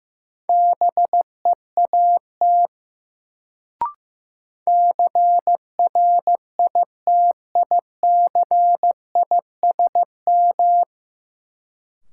Morse code, heard as a keyed tone, about 15 wpm, 700 Hz. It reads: BEAT CRITICISM
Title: Sentences from Top 1300 Words - New Word Drill - 15wpm